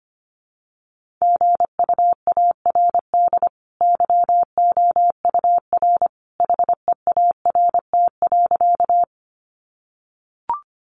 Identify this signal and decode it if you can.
Morse code, heard as a keyed tone, about 25 wpm, 700 Hz. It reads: GUARD YOUR HEART.
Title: Sentences from Top 1200 Words 25wpm